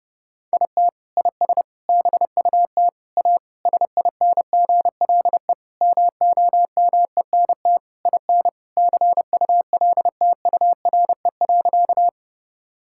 Morse code, heard as a keyed tone, about 32 wpm, 700 Hz. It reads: IT IS BUT A SINGLE MOMENT IN CULTURE.
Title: Sentences from Top 1000 Words - Review Part 6 - 30wpm